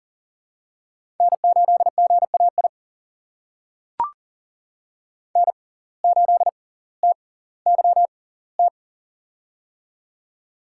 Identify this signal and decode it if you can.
Morse code, heard as a keyed tone, about 40 wpm, spaced as for 20 wpm, 700 Hz. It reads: N8GAI N8TYT
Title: USA Call Signs - 5 Characters - 6x Character Spacing - 40wpm